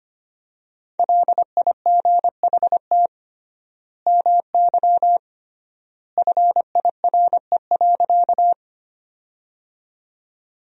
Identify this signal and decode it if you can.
Morse code, heard as a keyed tone, about 25 wpm, 700 Hz. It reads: LIGHT MY FIRE.
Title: 3x Word-Spacing - Sentences from Top 400 Words - 25wpm